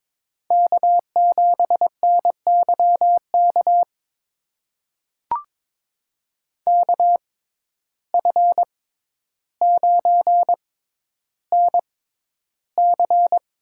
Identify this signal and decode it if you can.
Morse code, heard as a keyed tone, about 23 wpm, spaced as for 11 wpm, 700 Hz. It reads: K7NYK KF9NC
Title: USA Call Signs - 5 Characters - 6x Character Spacing - 22wpm